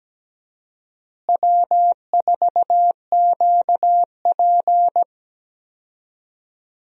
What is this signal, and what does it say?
Morse code, heard as a keyed tone, about 17 wpm, 700 Hz. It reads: W4QP
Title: USA Call Signs - 4 Characters - Repeated 3x - 17wpm